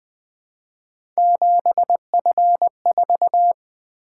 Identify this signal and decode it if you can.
Morse code, heard as a keyed tone, about 20 wpm, 700 Hz. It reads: 7F4
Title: Three Letter-Number Combinations - 20wpm